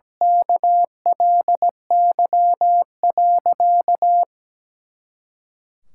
Morse code, heard as a keyed tone, about 17 wpm, 700 Hz. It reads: KLY.
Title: Pangram Challenge v2 - 17wpm